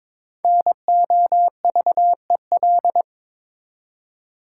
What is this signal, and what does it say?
Morse code, heard as a keyed tone, about 22 wpm, 690 Hz. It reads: NOVEL
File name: Difficult Words with Hints - 22wpm